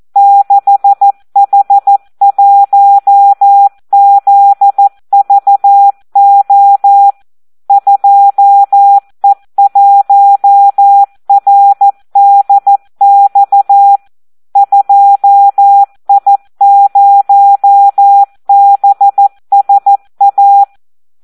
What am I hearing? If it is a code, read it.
Morse code, heard as a keyed tone, about 14 wpm, 800 Hz. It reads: 6H1ZVO 2E1RDX 2I0BSA
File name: Audio practica CW / SD 1 14